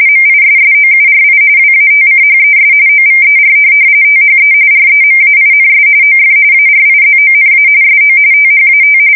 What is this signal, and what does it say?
RTTY, about 45.45 baud. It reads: MISSOURI WATCHES OVER THE REMAINS OF ARIZONA SO THAT
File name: fsk170 / USS-Missouri